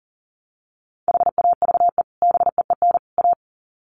Morse code, heard as a keyed tone, about 40 wpm, 700 Hz. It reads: HAVE BEEN A
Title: Sets of 3 Words 40wpm